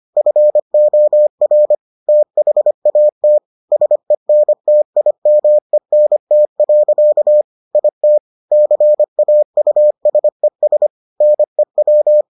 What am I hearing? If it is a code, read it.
Morse code, heard as a keyed tone, about 25 wpm, 600 Hz. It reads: FOR THAT SENTIMENT. IT CAUSES NEW